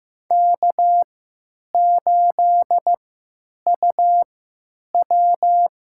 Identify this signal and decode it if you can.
Morse code, heard as a keyed tone, about 15 wpm, spaced as for 9 wpm, 700 Hz. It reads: K8UW
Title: USA Call Signs - 3x Character Spacing - 15wpm